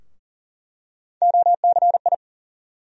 Morse code, heard as a keyed tone, about 40 wpm, 700 Hz. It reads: OCI
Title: USA Call Sign Postfixes - 3 Letters 40wpm